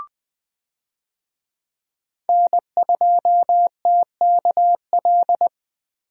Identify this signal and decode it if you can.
Morse code, heard as a keyed tone, about 20 wpm, 700 Hz. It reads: N2TKL
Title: USA Call Signs - 5 Characters - 20wpm